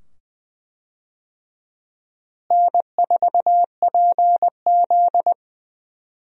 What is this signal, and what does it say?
Morse code, heard as a keyed tone, about 20 wpm, 700 Hz. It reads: N4PZ
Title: USA Call Signs - 4 Characters - Repeated 2x - 20wpm